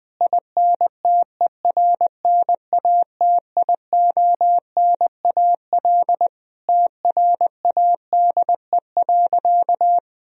Morse code, heard as a keyed tone, about 20 wpm, 700 Hz. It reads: INTERNATIONAL TRADE.
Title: Sentences from Top 1000 Words - Review Part 4 - 20wpm